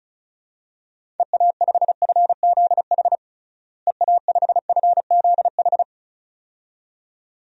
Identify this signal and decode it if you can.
Morse code, heard as a keyed tone, about 36 wpm, 700 Hz. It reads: EA5FZH EA5FZH
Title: DXCC-100 Call Signs - 6 Characters - Repeated 2x - 35wpm